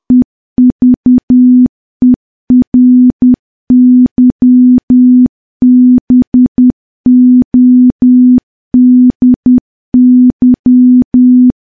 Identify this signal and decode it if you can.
Morse code, heard as a keyed tone, about 10 wpm, 250 Hz. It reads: EVERYBODY
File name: morse / morse-training-10WPM